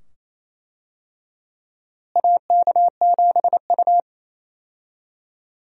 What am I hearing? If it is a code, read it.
Morse code, heard as a keyed tone, about 28 wpm, 700 Hz. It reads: AK7U
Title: USA Call Signs - 4 Characters - Repeated 2x - 28wpm